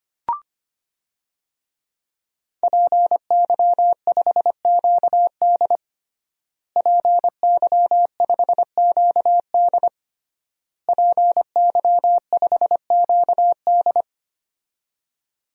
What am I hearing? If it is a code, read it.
Morse code, heard as a keyed tone, about 25 wpm, 700 Hz. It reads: PY5QD PY5QD PY5QD
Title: DXCC-100 Call Signs - 5 Characters - Repeated 3x - 25wpm